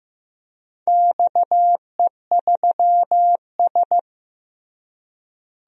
Morse code, heard as a keyed tone, about 15 wpm, 700 Hz. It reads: XE3S
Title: DXCC-100 Call Signs - 4 Characters - Repeated 3x - 15wpm